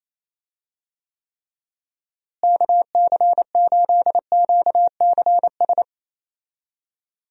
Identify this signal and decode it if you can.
Morse code, heard as a keyed tone, about 28 wpm, 700 Hz. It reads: KC8QCS
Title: USA Call Signs - 6 Characters - Repeated 3x - 28wpm